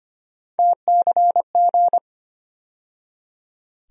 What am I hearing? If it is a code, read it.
Morse code, heard as a keyed tone, about 25 wpm, 700 Hz. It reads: TCG
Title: USA Call Sign Postfixes - 3 Letters 25wpm